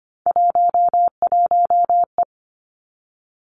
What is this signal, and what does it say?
Morse code, heard as a keyed tone, about 25 wpm, 700 Hz. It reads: 11E
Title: Three Challenging Letter-Number Combinations 25wpm